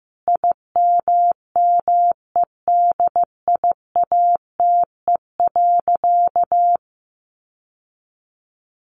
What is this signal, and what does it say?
Morse code, heard as a keyed tone, about 15 wpm, 700 Hz. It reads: IMMEDIATE.
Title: Sentences from Top 1200 Words 15wpm